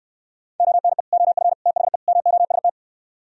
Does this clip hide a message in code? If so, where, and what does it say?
Yes: Morse code from 0.6 to 2.7 s, heard as a keyed tone, about 63 wpm, 700 Hz. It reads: ONE OF THE MOST